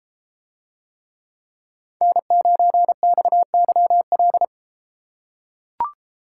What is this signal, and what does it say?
Morse code, heard as a keyed tone, about 33 wpm, 700 Hz. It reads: N9XYL